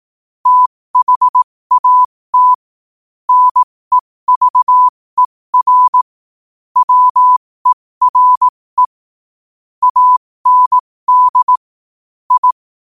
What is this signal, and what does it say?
Morse code, heard as a keyed tone, about 18 wpm, spaced as for 15 wpm, 1000 Hz. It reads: THAT NEVER WERE AND I
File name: qotdfiles / 15wpm